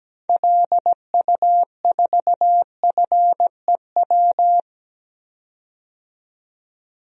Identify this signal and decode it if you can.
Morse code, heard as a keyed tone, about 17 wpm, 700 Hz. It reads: LU4FEW
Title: DXCC-100 Call Signs - 6 Characters - 17wpm